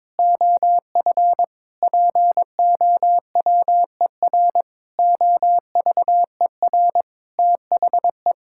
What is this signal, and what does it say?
Morse code, heard as a keyed tone, about 21 wpm, 700 Hz. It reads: OF POWER OVER THE
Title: Sentences from Top 1000 Words - Review Part 5 - 22wpm